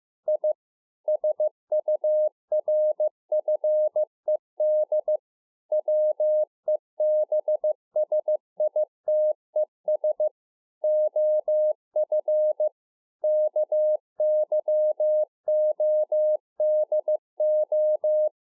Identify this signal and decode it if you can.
Morse code, heard as a keyed tone, about 15 wpm, 610 Hz. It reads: I SURFED WEBSITES OF KYODO
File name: 2011-04-mp3 / THE 3.11 6 15WPM